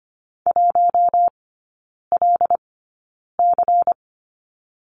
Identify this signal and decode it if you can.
Morse code, heard as a keyed tone, about 25 wpm, spaced as for 9 wpm, 700 Hz. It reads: 1LC
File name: course / 25:10wpm / 093 Lesson 16e - Letter c - Three Characters 25wpm